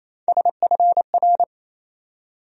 Morse code, heard as a keyed tone, about 28 wpm, 700 Hz. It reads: SFR
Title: USA Call Sign Postfixes - 3 Letters 28wpm